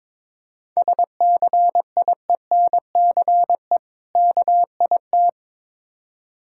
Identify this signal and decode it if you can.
Morse code, heard as a keyed tone, about 22 wpm, 690 Hz. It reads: SCIENCE KIT
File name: Difficult Words with Hints - 22wpm